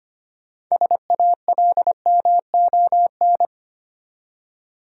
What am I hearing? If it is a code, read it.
Morse code, heard as a keyed tone, about 25 wpm, 700 Hz. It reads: SALMON
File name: practice / 25wpm / Easy Words with Hints - 25wpm